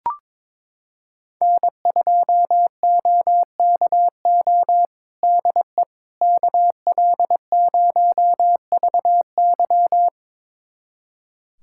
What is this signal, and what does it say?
Morse code, heard as a keyed tone, about 22 wpm, 700 Hz. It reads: N2OKO DE KL0VY
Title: USA Callsign DE Callsign 22wpm